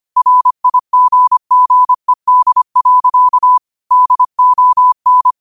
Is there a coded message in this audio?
Yes: Morse code from 0.2 to 5.3 s, heard as a keyed tone, about 25 wpm, 1000 Hz. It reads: RIGGED. DON